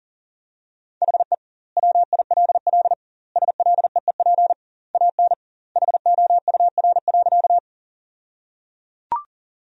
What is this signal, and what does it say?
Morse code, heard as a keyed tone, about 37 wpm, 700 Hz. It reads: HE WILL SLEEP AN HOUR.
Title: 2x Word-Spacing - Sentences from Top 500 Words - 40wpm